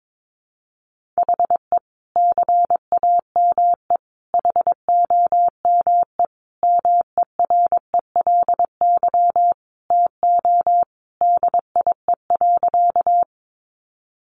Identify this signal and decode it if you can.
Morse code, heard as a keyed tone, about 22 wpm, 700 Hz. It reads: HE CAME HOME MERELY TO DIE.
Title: Sentences from Top 1400 Words 22wpm